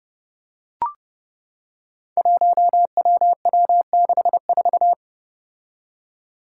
Morse code, heard as a keyed tone, about 30 wpm, 700 Hz. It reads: 1WW64
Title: Five Challenging Letter-Number Combinations 30wpm